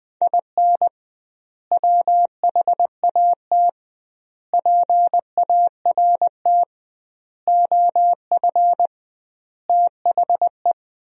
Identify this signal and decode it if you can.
Morse code, heard as a keyed tone, about 20 wpm, 700 Hz. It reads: IN WHAT PART OF THE
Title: 2x Word-Spacing - Sentences from Top 200 Words - 20wpm